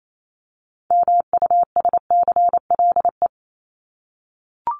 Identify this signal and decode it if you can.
Morse code, heard as a keyed tone, about 28 wpm, 700 Hz. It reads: MUSCLE
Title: Top 1251-1500 Words 28wpm